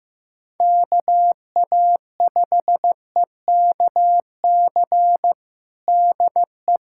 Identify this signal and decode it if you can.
Morse code, heard as a keyed tone, about 15 wpm, 700 Hz. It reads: KA5EKC DE
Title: USA Callsign DE Callsign - 6 Characters - 15wpm